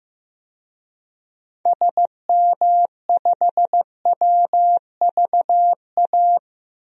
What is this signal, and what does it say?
Morse code, heard as a keyed tone, about 14 wpm, 700 Hz. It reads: SM5WVA
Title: DXCC-100 Call Signs - 6 Characters - 15wpm